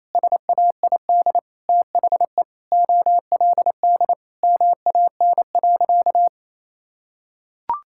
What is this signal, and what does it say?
Morse code, heard as a keyed tone, about 27 wpm, 700 Hz. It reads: SAID THE OLD MAN.
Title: Sentences from Top 1600 Words 28wpm